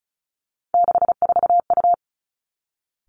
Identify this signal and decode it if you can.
Morse code, heard as a keyed tone, about 36 wpm, 700 Hz. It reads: 64U
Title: Three Challenging Letter-Number Combinations 35wpm